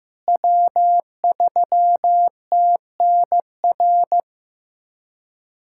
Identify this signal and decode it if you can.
Morse code, heard as a keyed tone, about 15 wpm, 700 Hz. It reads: W3TNR
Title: USA Callsign DE Callsign - 5 Characters - 15wpm